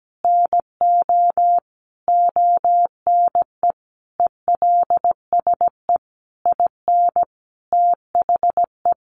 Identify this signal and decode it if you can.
Morse code, heard as a keyed tone, about 17 wpm, 700 Hz. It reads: NO ONE ELSE IN THE
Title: Sentences from Top 1100 Words 17wpm